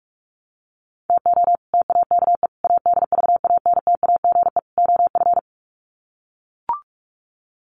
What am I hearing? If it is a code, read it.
Morse code, heard as a keyed tone, about 44 wpm, 700 Hz. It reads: TO TAKE ADVANTAGE OF
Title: Sets of 4 Words 45wpm